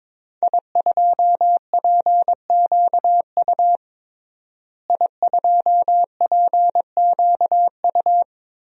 Morse code, heard as a keyed tone, about 22 wpm, 700 Hz. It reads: I2PQU I2PQU
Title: DXCC-100 Call Signs - 5 Characters - Repeated 3x - 22wpm